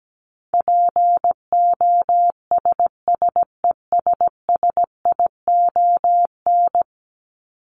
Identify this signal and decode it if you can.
Morse code, heard as a keyed tone, about 17 wpm, 700 Hz. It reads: POSSESSION
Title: Top 1251-1500 Words 17wpm